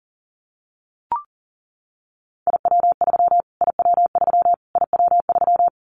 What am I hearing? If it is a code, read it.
Morse code, heard as a keyed tone, about 37 wpm, 700 Hz. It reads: IW3 IW3 IW3
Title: Callsign Prefixes for Top 25 DXCC - Repeated 3x - 40wpm